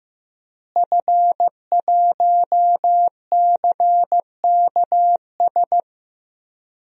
Morse code, heard as a keyed tone, about 15 wpm, 700 Hz. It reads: F1CKS
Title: DXCC-100 Call Signs - 5 Characters - Repeated 3x - 15wpm